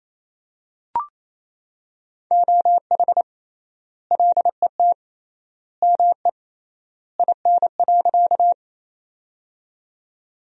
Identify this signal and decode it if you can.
Morse code, heard as a keyed tone, about 27 wpm, 700 Hz. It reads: OH LET ME IN.
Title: 3x Word-Spacing - Sentences from Top 500 Words - 28wpm